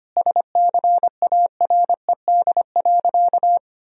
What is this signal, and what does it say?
Morse code, heard as a keyed tone, about 25 wpm, 700 Hz. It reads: SCARED.
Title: Sentences from Top 1000 Words - Review Part 4 - 25wpm